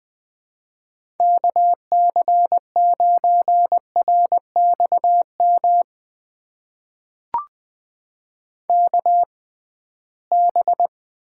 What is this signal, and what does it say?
Morse code, heard as a keyed tone, about 20 wpm, spaced as for 15 wpm, 700 Hz. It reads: KC9RXM KB
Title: USA Call Signs - 6 Characters - 6x Character Spacing - 20wpm